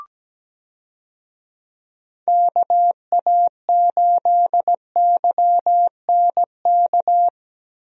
Morse code, heard as a keyed tone, about 17 wpm, 700 Hz. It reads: KA8YNK
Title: USA Call Signs - 6 Characters - Repeated 3x - 17wpm